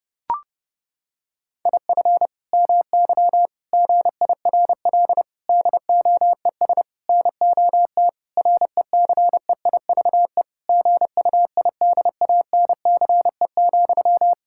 Morse code, heard as a keyed tone, about 30 wpm, 700 Hz. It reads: IF MY GIRL DOES NOT RECEIVE GUIDANCE,
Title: Sentences from Top 1600 Words 30wpm